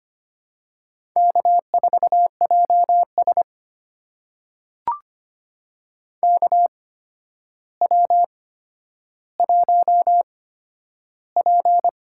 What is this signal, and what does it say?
Morse code, heard as a keyed tone, about 25 wpm, spaced as for 11 wpm, 700 Hz. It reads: K4JS KW1P
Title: USA Call Signs - 4 Characters - 8x Character Spacing - 25wpm